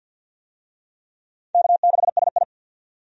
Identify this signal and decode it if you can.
Morse code, heard as a keyed tone, about 50 wpm, 700 Hz. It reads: K6SI
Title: USA Call Signs - 4 Characters - 50wpm